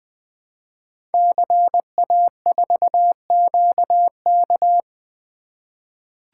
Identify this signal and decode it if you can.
Morse code, heard as a keyed tone, about 20 wpm, 700 Hz. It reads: CA4QK
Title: Code Groups - 5 Letter:Numbers - 20wpm